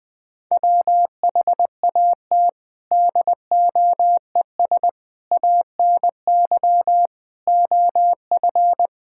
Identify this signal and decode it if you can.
Morse code, heard as a keyed tone, about 21 wpm, 700 Hz. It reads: WHAT DOES ANY OF
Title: Sets of 9 Words - 20wpm